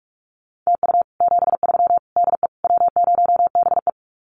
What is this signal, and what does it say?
Morse code, heard as a keyed tone, about 44 wpm, 700 Hz. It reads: TU 73 DE W0BE
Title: Common QSO Phrases Encore 45wpm